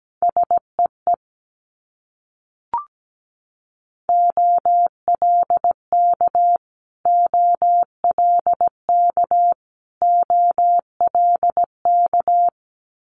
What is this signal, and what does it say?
Morse code, heard as a keyed tone, about 17 wpm, 700 Hz. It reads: SEE OLK OLK OLK
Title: Multi-letter Phonemes Repeated 3x 17wpm